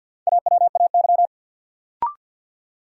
Morse code, heard as a keyed tone, about 52 wpm, 700 Hz. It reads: AWAY
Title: Sentences from Top 600 Words - New Word Drill - 50wpm